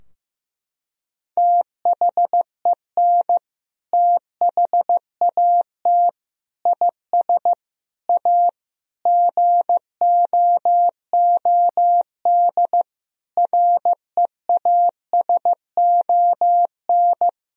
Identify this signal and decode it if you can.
Morse code, heard as a keyed tone, about 15 wpm, 700 Hz. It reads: THEN THAT IS A GOOD REASON
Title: Sentences from Top 1000 Words - Review Part 4 - 15wpm